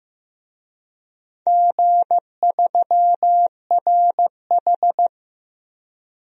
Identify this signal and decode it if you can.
Morse code, heard as a keyed tone, about 15 wpm, 700 Hz. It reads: G3RH
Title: DXCC-100 Call Signs - 4 Characters - Repeated 3x - 15wpm